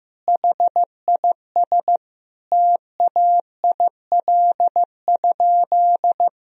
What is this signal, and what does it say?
Morse code, heard as a keyed tone, about 15 wpm, 700 Hz. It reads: HIS TAIL?
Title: Sentences from Top 1000 Words - Review Part 3 - 15wpm